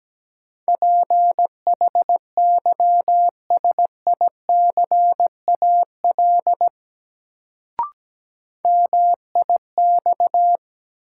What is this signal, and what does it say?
Morse code, heard as a keyed tone, about 17 wpm, 700 Hz. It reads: PHYSICAL MIX
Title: Sentences from Top 1000 Words - Review Part 4 - New Word Drill - 17wpm